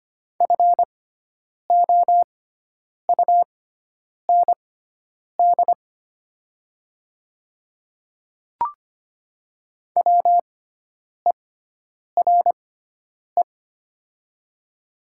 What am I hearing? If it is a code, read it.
Morse code, heard as a keyed tone, about 25 wpm, spaced as for 9 wpm, 700 Hz. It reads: FOUND WERE
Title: 6x Character Spacing - Top 500 Words - 25wpm